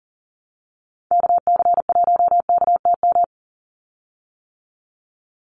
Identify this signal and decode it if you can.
Morse code, heard as a keyed tone, about 39 wpm, 700 Hz. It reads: KC1KTM
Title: USA Call Signs 40wpm